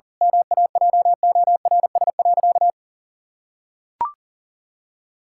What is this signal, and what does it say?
Morse code, heard as a keyed tone, about 40 wpm, 700 Hz. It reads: MAJORS.
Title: Pangram Challenge v2 - 40wpm